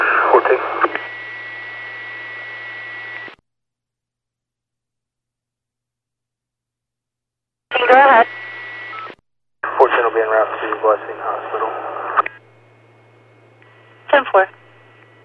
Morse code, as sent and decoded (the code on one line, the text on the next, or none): none